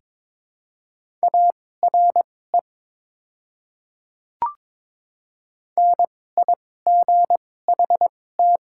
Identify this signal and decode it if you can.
Morse code, heard as a keyed tone, about 22 wpm, spaced as for 16 wpm, 690 Hz. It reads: ARE NIGHT
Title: 2x Character Spacing - Top 500 Words - 22wpm